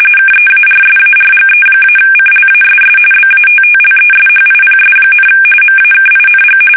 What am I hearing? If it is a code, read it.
RTTY, about 45.45 baud. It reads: KEARSARGE-CLASS BATTLESHIPS, MAKING
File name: fsk850 / battleships